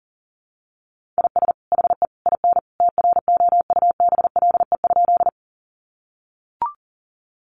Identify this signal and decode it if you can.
Morse code, heard as a keyed tone, about 37 wpm, 700 Hz. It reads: IS HE IN TROUBLE?
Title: Sentences from Top 1200 Words 40wpm